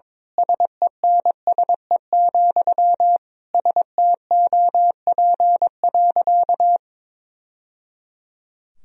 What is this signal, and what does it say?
Morse code, heard as a keyed tone, about 22 wpm, 700 Hz. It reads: SENSE, STOP.
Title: Sentences from Top 1000 Words - Review Part 4 - 22wpm